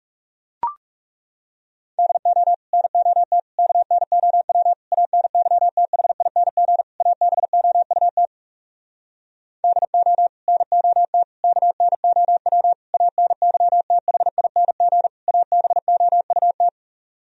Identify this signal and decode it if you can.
Morse code, heard as a keyed tone, about 40 wpm, 700 Hz. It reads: DO NOT KNOW ANYTHING ABOUT DO NOT KNOW ANYTHING ABOUT